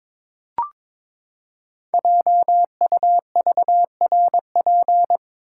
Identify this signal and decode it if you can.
Morse code, heard as a keyed tone, about 22 wpm, 700 Hz. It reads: JUVRP
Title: Code Groups - 5 Letters - 22wpm